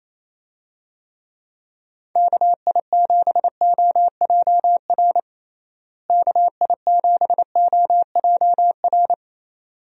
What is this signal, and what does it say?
Morse code, heard as a keyed tone, about 28 wpm, 700 Hz. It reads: KI7OJR KI7OJR
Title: USA Call Signs - 6 Characters - Repeated 2x - 28wpm